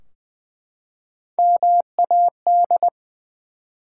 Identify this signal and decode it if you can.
Morse code, heard as a keyed tone, about 20 wpm, 700 Hz. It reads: MAD
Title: Three Letter Words - 20wpm